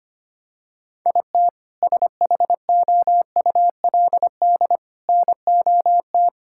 Morse code, heard as a keyed tone, about 25 wpm, 700 Hz. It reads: IT SHOULD NOT
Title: Sentences from Top 1600 Words 25wpm